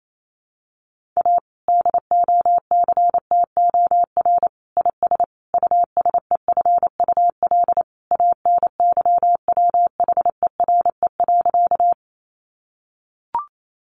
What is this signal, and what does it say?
Morse code, heard as a keyed tone, about 28 wpm, 700 Hz. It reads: A DOCTOR IS USEFUL ANYWHERE.
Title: Sentences from Top 1500 Words 28wpm